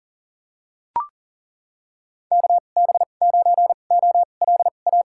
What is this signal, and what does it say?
Morse code, heard as a keyed tone, about 40 wpm, spaced as for 29 wpm, 700 Hz. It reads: KB9OLA
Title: USA Call Signs - 6 Characters - 2x Character Spacing - 40wpm